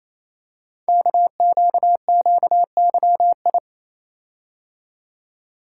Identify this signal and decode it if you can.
Morse code, heard as a keyed tone, about 29 wpm, 700 Hz. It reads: KQQYI
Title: Code Groups - 5 Letter:Numbers - 28wpm